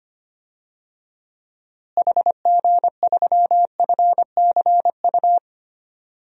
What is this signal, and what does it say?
Morse code, heard as a keyed tone, about 25 wpm, 700 Hz. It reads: HG3FCU
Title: DXCC-100 Call Signs - 6 Characters - Repeated 3x - 25wpm